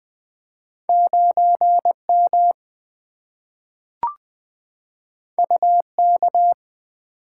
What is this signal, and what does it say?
Morse code, heard as a keyed tone, about 20 wpm, 700 Hz. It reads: 9 M U K
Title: Two Letter-Number Combinations - 20wpm